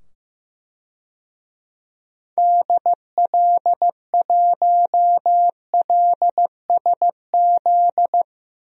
Morse code, heard as a keyed tone, about 15 wpm, 700 Hz. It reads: DL1LSZ
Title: DXCC-100 Call Signs - 6 Characters - 15wpm